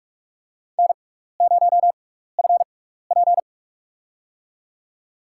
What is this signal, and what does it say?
Morse code, heard as a keyed tone, about 43 wpm, spaced as for 15 wpm, 700 Hz. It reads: N0FP
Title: USA Call Signs - 4 Characters - 6x Character Spacing - 45wpm